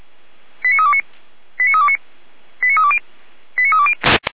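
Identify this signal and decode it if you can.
Five-tone selective call (ZVEI), heard as quick runs of five tones, tones of about 70 ms. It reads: 89329 89329 89320 89320
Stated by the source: ZVEI five-tone selective call 89504 received on SDR.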